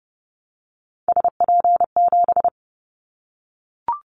Morse code, heard as a keyed tone, about 30 wpm, 700 Hz. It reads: SP7
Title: Callsign Prefixes for Top 25 DXCC - Repeated 3x - 30wpm